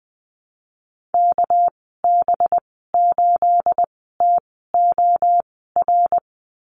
Morse code, heard as a keyed tone, about 20 wpm, spaced as for 14 wpm, 700 Hz. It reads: KB8TOR
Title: 219 Lesson 35g - Number 0 - Call Signs 20wpm